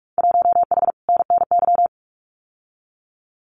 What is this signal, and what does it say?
Morse code, heard as a keyed tone, about 46 wpm, 700 Hz. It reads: 1H NNY
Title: Winter Field Day Exchanges 45wpm